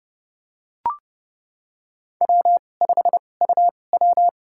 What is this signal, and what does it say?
Morse code, heard as a keyed tone, about 30 wpm, spaced as for 22 wpm, 700 Hz. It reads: W5UW
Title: USA Call Signs - 2x Character Spacing - 30wpm